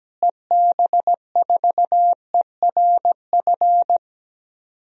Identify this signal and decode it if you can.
Morse code, heard as a keyed tone, about 17 wpm, 700 Hz. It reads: EB4ERF
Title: DXCC-100 Call Signs - 6 Characters - Repeated 2x - 17wpm